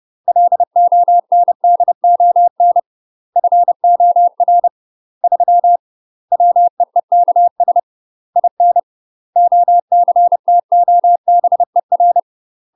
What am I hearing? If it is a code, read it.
Morse code, heard as a keyed tone, about 30 wpm, 700 Hz. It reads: LONDON FOR 3 WEEKS IN OCTOBER
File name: UK MASTER QSOs TRAINING/30WPM